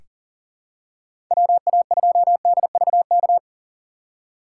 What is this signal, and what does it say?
Morse code, heard as a keyed tone, about 40 wpm, 700 Hz. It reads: WA2DVK
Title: USA Call Signs - 6 Characters - 3x Character Spacing - 40wpm